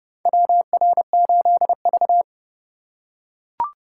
Morse code, heard as a keyed tone, about 29 wpm, 700 Hz. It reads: WR8V
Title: USA Call Signs - 4 Characters - 2x Character Spacing - 30wpm